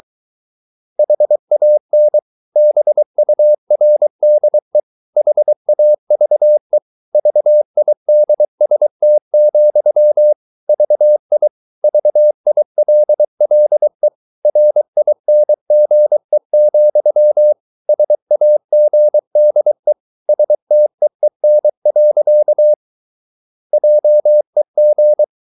Morse code, heard as a keyed tone, about 23 wpm, 600 Hz. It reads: HAN BURDE HAVE VIDST, VI VILLE RINGE, SAGDE STEEN. JEG
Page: Ferie 23 wpm